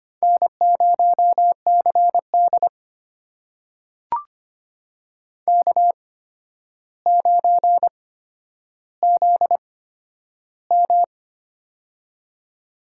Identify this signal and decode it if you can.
Morse code, heard as a keyed tone, about 25 wpm, spaced as for 11 wpm, 700 Hz. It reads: N0CD K9ZM
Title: USA Call Signs - 4 Characters - 8x Character Spacing - 25wpm